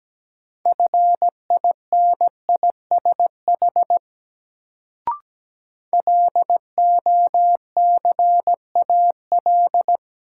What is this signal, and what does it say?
Morse code, heard as a keyed tone, about 17 wpm, 700 Hz. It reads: FINISH LOCAL
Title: Sentences from Top 1000 Words - Review Part 3 - New Word Drill - 17wpm